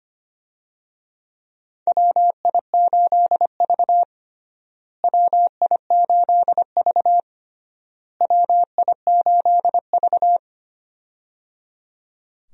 Morse code, heard as a keyed tone, about 25 wpm, 700 Hz. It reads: WI8V WI8V WI8V
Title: USA Call Signs - 4 Characters - Repeated 3x - 25wpm